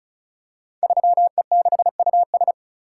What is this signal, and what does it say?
Morse code, heard as a keyed tone, about 36 wpm, 700 Hz. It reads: 3E6US
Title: Code Groups - 5 Letter:Numbers - 35wpm